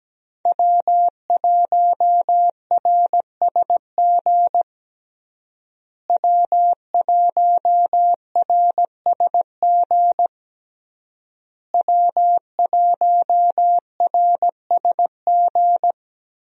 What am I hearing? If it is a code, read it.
Morse code, heard as a keyed tone, about 17 wpm, 700 Hz. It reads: W1RSG W1RSG W1RSG
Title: USA Call Signs - 5 Characters - Repeated 3x - 17wpm